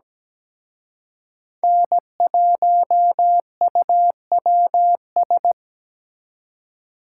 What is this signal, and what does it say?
Morse code, heard as a keyed tone, about 20 wpm, 700 Hz. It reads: N1UWS
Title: Speed Racing - USA 1x3 Call Signs 17wpm